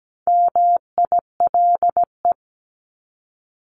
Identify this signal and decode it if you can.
Morse code, heard as a keyed tone, about 17 wpm, 700 Hz. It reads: MILE
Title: ES:S Suffix - Top 100 Words with Base Words 17wpm